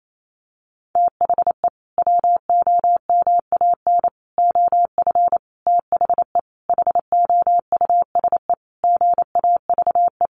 Morse code, heard as a keyed tone, about 28 wpm, 700 Hz. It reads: THE WOMAN OF THE HOUSE GAVE
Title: Sentences from Top 1500 Words 28wpm